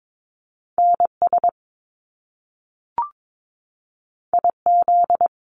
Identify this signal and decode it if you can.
Morse code, heard as a keyed tone, about 22 wpm, 700 Hz. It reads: NS IZ
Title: Top Two Letter Combinations in Words 22wpm